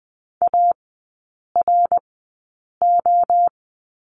Morse code, heard as a keyed tone, about 20 wpm, spaced as for 8 wpm, 700 Hz. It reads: ARO
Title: 053 Lesson 10e - Letter r - Three Characters 20wpm